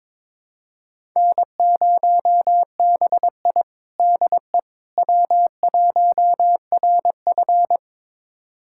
Morse code, heard as a keyed tone, about 22 wpm, 700 Hz. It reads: N0BI DE W1RF
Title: USA Callsign DE Callsign - 4 Characters - 22wpm